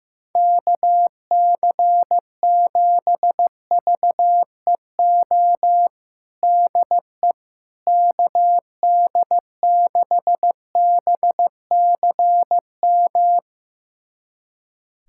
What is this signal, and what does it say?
Morse code, heard as a keyed tone, about 16 wpm, 700 Hz. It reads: KC7VEO DE KD6BCM
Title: USA Callsign DE Callsign - 6 Characters - 15wpm